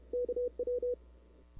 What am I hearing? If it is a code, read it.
Morse code, heard as a keyed tone, about 30 wpm, 490 Hz. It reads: KW
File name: CW / 2005-k5tr arrl cw dx -00:00